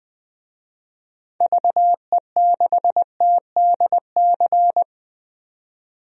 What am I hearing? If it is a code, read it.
Morse code, heard as a keyed tone, about 21 wpm, 700 Hz. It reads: VE6TDC
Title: DXCC-100 Call Signs - 6 Characters - 20wpm